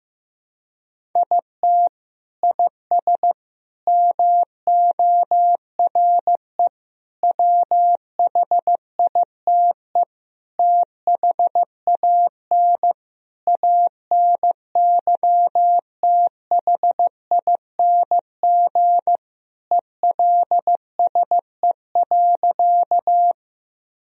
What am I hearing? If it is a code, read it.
Morse code, heard as a keyed tone, about 15 wpm, 700 Hz. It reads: IT IS MORE WHITE THAN ANYTHING ELSE.
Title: Sentences from Top 1000 Words - Review Part 1 - 15wpm